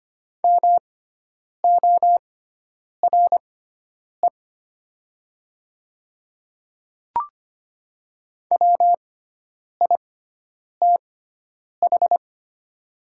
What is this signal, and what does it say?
Morse code, heard as a keyed tone, about 25 wpm, spaced as for 9 wpm, 700 Hz. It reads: MORE WITH
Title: 6x Character Spacing - Top 200 Words - 25wpm